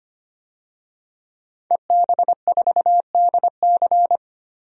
Morse code, heard as a keyed tone, about 25 wpm, 700 Hz. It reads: EB4DC
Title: DXCC-100 Call Signs - 5 Characters - 25wpm